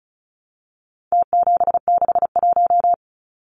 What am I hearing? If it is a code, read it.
Morse code, heard as a keyed tone, about 34 wpm, 700 Hz. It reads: T761
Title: Numbers - Contest Style - 4-Digit 35wpm